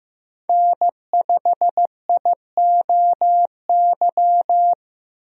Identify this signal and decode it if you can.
Morse code, heard as a keyed tone, about 15 wpm, 700 Hz. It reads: N5IOY
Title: USA Call Signs - 5 Characters - 15wpm